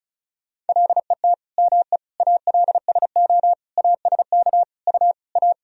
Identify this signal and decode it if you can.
Morse code, heard as a keyed tone, about 36 wpm, 700 Hz. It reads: LET ME ALSO ASK U A
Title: Sentences from Top 400 Words and Abbreviations 35wpm